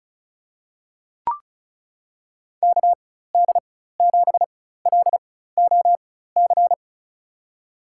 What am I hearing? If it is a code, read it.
Morse code, heard as a keyed tone, about 34 wpm, spaced as for 16 wpm, 700 Hz. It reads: KD7LOC
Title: USA Call Signs - 6 Characters - 4x Character Spacing - 35wpm